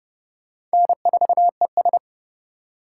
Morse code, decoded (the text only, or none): N4ES